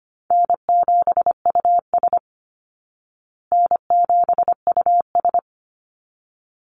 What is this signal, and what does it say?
Morse code, heard as a keyed tone, about 25 wpm, 700 Hz. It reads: N7US N7US
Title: Call Signs Repeated 3x - 25wpm